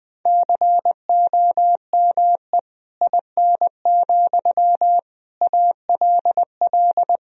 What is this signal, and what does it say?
Morse code, heard as a keyed tone, about 21 wpm, 700 Hz. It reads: COME IN, ALL
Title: Sentences from Top 1000 Words - Review Part 1 - 20wpm